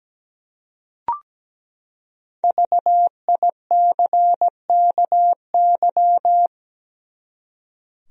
Morse code, heard as a keyed tone, about 17 wpm, 700 Hz. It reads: VICKY
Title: USA Common Names - Max 5 Letters - 17wpm